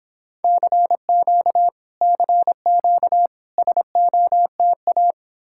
Morse code, heard as a keyed tone, about 29 wpm, 700 Hz. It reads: CQ CQ SOTA